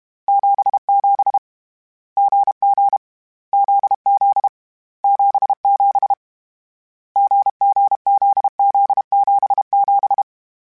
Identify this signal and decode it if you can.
Morse code, heard as a keyed tone, about 32 wpm, 800 Hz. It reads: 77 GG ZZ 77 GGZZ77
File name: Short Runs / Short Runs-mp3s / GZ GZ7 @30